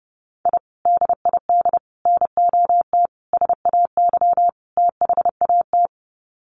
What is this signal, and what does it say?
Morse code, heard as a keyed tone, about 30 wpm, 700 Hz. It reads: I DID NOT SAY THAT
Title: Sets of 5 Words Encore2 30wpm